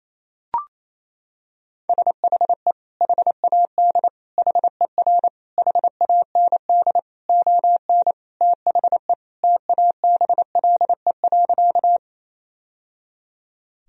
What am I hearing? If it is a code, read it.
Morse code, heard as a keyed tone, about 27 wpm, 690 Hz. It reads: SHE HAD HER HAND ON THE TABLE.
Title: Sentences from Top 1000 Words - Review Part 2 - 28wpm